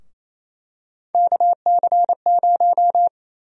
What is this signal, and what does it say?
Morse code, heard as a keyed tone, about 28 wpm, 700 Hz. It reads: KC0
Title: USA Call Sign Prefixes - Repeated 2x - 28wpm